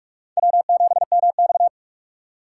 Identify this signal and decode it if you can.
Morse code, heard as a keyed tone, about 46 wpm, 690 Hz. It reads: W7MX
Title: USA Call Signs - 4 Characters - 4x Character Spacing - 45wpm